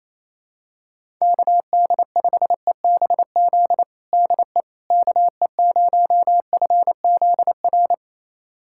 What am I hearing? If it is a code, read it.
Morse code, heard as a keyed tone, about 28 wpm, 700 Hz. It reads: KD5EBZ DE KE0FZR
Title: USA Callsign DE Callsign - 6 Characters - 28wpm